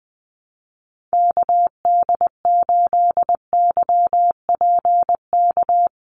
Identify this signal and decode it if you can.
Morse code, heard as a keyed tone, about 20 wpm, 700 Hz. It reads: KD8YPK
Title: USA Call Signs 20wpm